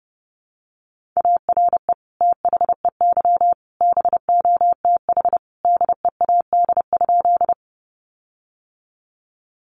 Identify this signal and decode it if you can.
Morse code, heard as a keyed tone, about 30 wpm, 700 Hz. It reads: ARE THEY BOTH DEAD?
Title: Sentences from Top 1500 Words 30wpm